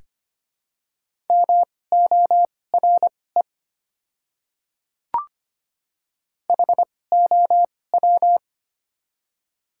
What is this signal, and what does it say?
Morse code, heard as a keyed tone, about 25 wpm, spaced as for 18 wpm, 700 Hz. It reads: MORE HOW
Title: 2x Character Spacing - Top 100 Words - 25wpm